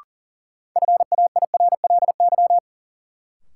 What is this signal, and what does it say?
Morse code, heard as a keyed tone, about 37 wpm, 700 Hz. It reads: FAIRLY
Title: Sentences from Top 1100 Words - New Word Drill - 40wpm